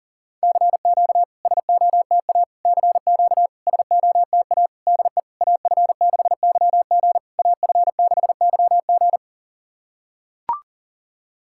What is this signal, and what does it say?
Morse code, heard as a keyed tone, about 40 wpm, 700 Hz. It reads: CQ SOTA CQ SOTA DE AF6YG AF6YG
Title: SOTA QSO Practice 40wpm